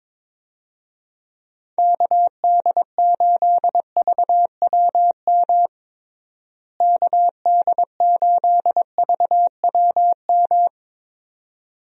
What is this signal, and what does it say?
Morse code, heard as a keyed tone, about 22 wpm, 700 Hz. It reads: KD8VWM KD8VWM
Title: USA Call Signs - 6 Characters - Repeated 2x - 22wpm